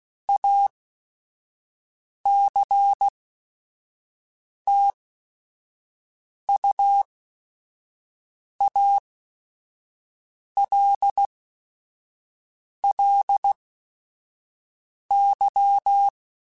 Morse code, heard as a keyed tone, about 16 wpm, spaced as for 5 wpm, 800 Hz. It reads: ACTUALLY